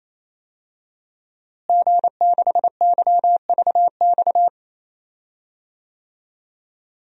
Morse code, heard as a keyed tone, about 28 wpm, 700 Hz. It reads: G6YVX
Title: DXCC-100 Call Signs - 5 Characters - Repeated 3x - 28wpm